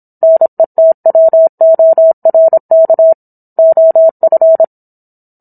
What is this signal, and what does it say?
Morse code, heard as a keyed tone, about 27 wpm, 650 Hz. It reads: NETWORK OF